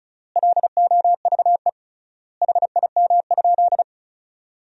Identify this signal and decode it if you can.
Morse code, heard as a keyed tone, about 36 wpm, 700 Hz. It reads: LOVE HIM?
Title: 3x Word-Spacing - Sentences from Top 400 Words - 35wpm